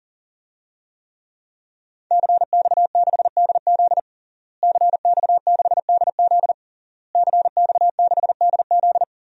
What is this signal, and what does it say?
Morse code, heard as a keyed tone, about 40 wpm, 700 Hz. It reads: CX6DZ CX6DZ CX6DZ
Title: DXCC-100 Call Signs - 5 Characters - Repeated 3x - 40wpm